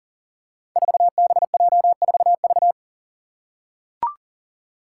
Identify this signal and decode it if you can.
Morse code, heard as a keyed tone, about 40 wpm, 700 Hz. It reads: 4BJ4V
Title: Five Challenging Letter-Number Combinations 40wpm